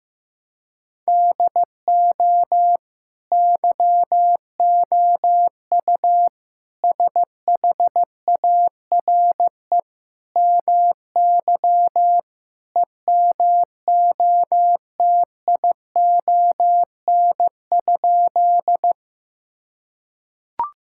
Morse code, heard as a keyed tone, about 15 wpm, 700 Hz. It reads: DO YOU SHARE MY EMOTION?
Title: Sentences from Top 1600 Words 15wpm